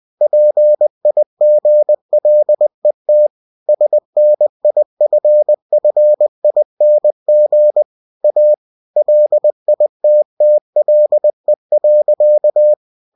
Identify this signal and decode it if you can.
Morse code, heard as a keyed tone, about 20 wpm, 600 Hz. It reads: PIGLET SNIFFING A LITTLE.
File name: cw-books / winnie-the-pooh / winnie-20wpm-page-0056